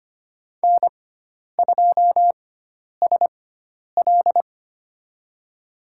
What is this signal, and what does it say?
Morse code, heard as a keyed tone, about 25 wpm, spaced as for 10 wpm, 700 Hz. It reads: N2SL
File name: Lesson 16g - Letter c - Call Signs 25wpm